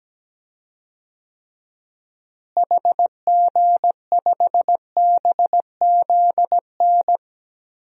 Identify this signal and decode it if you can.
Morse code, heard as a keyed tone, about 17 wpm, 700 Hz. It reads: HG5BZN
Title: DXCC-100 Call Signs - 6 Characters - Repeated 3x - 17wpm